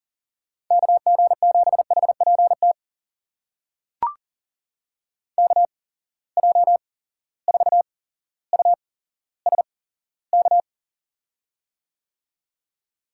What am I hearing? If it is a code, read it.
Morse code, heard as a keyed tone, about 39 wpm, spaced as for 17 wpm, 700 Hz. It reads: KG7HPT KJ4USK
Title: USA Call Signs - 6 Characters - 8x Character Spacing - 40wpm